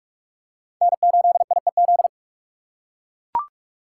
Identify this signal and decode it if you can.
Morse code, heard as a keyed tone, about 42 wpm, 700 Hz. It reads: N8IEZ